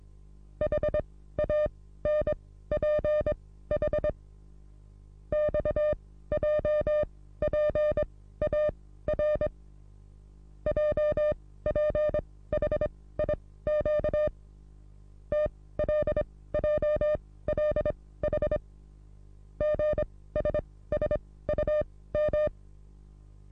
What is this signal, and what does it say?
Morse code, heard as a keyed tone, about 22 wpm, spaced as for 15 wpm, 600 Hz. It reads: HANPH XJPAR JPHIQ TLJLH GSSUM
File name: repos / Morsekurs / DJ4UF Morsen27